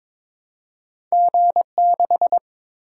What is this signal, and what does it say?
Morse code, heard as a keyed tone, about 22 wpm, 700 Hz. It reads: G6
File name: practice / 22wpm / Two Letter-Number Combinations - 22wpm